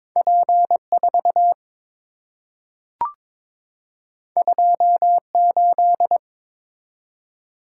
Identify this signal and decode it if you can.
Morse code, heard as a keyed tone, about 23 wpm, 700 Hz. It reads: P4 28
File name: practice / 22wpm / Two Letter-Number Combinations - 22wpm